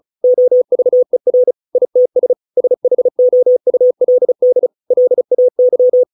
Morse code, heard as a keyed tone, about 36 wpm, 500 Hz. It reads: OVER ITS SHOULD LAY